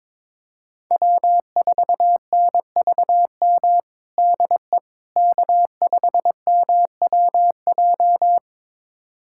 Morse code, heard as a keyed tone, about 21 wpm, 700 Hz. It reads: W4NVM DE K5MWJ
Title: USA Callsign DE Callsign - 5 Characters - 22wpm